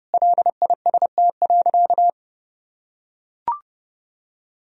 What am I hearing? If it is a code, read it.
Morse code, heard as a keyed tone, about 30 wpm, 700 Hz. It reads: LIST.
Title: Sentences from Top 1000 Words - Review Part 2 - 30wpm